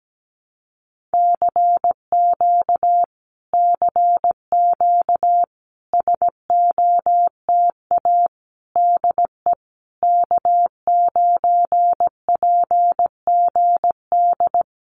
Morse code, heard as a keyed tone, about 17 wpm, 700 Hz. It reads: CQ CQ SOTA DE K9PGD
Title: Common QSO Phrases Encore 17wpm